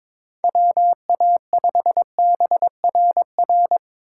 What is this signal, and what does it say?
Morse code, heard as a keyed tone, about 22 wpm, 700 Hz. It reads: WA5BRR
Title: USA Call Signs - 6 Characters - 2x Character Spacing - 22wpm